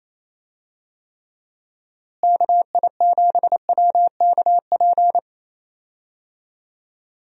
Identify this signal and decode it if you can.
Morse code, heard as a keyed tone, about 28 wpm, 700 Hz. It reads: KI7WKP
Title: USA Call Signs - 6 Characters - Repeated 2x - 28wpm